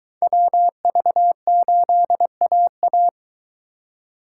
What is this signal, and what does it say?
Morse code, heard as a keyed tone, about 23 wpm, 700 Hz. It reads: WV8AA